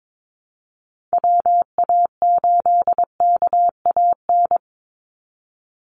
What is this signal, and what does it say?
Morse code, heard as a keyed tone, about 22 wpm, 700 Hz. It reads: WA8KAN
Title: Call Signs Repeated 3x - 22wpm